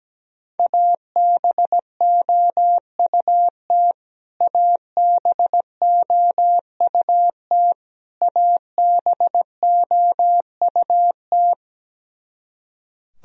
Morse code, heard as a keyed tone, about 17 wpm, 700 Hz. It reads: ABOUT ABOUT ABOUT
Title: Top 100 Words Repeated 17wpm